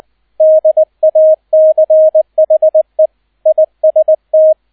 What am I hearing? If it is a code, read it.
Morse code, heard as a keyed tone, about 20 wpm, 630 Hz, with radio noise. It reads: DACHE IST
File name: 002 Zweiter Streich 20wpm